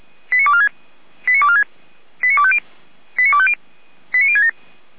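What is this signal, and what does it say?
Five-tone selective call (ZVEI), heard as quick runs of five tones, tones of about 70 ms. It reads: 89246 89246 89250 89250 89067
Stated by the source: ZVEI five-tone selective call received by SDR, fire brigade Orlamünde (Feuerwehr Orlamünde) station f85360.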